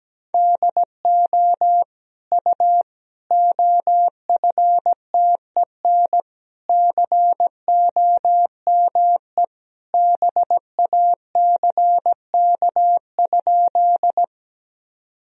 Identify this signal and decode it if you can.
Morse code, heard as a keyed tone, about 17 wpm, 700 Hz. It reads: DO U OFTEN COME BACK?
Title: Sentences from Top 300 Words and Abbreviations 17wpm